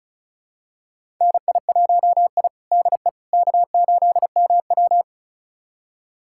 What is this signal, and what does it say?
Morse code, heard as a keyed tone, about 36 wpm, 700 Hz. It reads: NI1I DE K8MW
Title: USA Callsign DE Callsign - 4 Characters - 35wpm